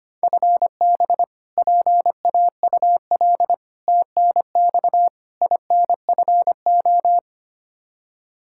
Morse code, heard as a keyed tone, about 25 wpm, 700 Hz. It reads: FB PAUL TNX INFO
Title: Practice QSO - Exchange 3 - More Info - 25wpm